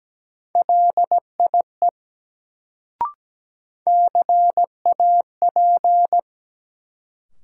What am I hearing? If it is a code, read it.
Morse code, heard as a keyed tone, about 17 wpm, 700 Hz. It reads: LIE CAP
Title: Sentences from Top 1600 Words - New Word Drill - 17wpm